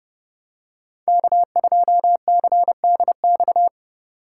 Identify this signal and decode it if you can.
Morse code, heard as a keyed tone, about 32 wpm, 700 Hz. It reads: K2CDX
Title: K1USN SST QSO Practice - 30wpm